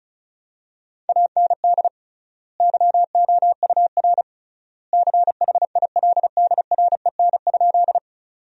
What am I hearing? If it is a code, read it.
Morse code, heard as a keyed tone, about 36 wpm, 700 Hz. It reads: AND YOUR CHILDREN?
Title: 3x Word-Spacing - Sentences from Top 300 Words - 35wpm